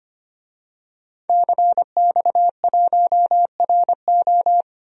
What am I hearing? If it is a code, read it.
Morse code, heard as a keyed tone, about 25 wpm, 700 Hz. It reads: CX1RO
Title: DXCC-100 Call Signs - 5 Characters - 25wpm